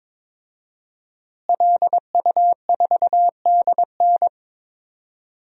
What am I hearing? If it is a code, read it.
Morse code, heard as a keyed tone, about 22 wpm, 700 Hz. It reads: LU4DN
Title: DXCC-100 Call Signs - 5 Characters - Repeated 3x - 22wpm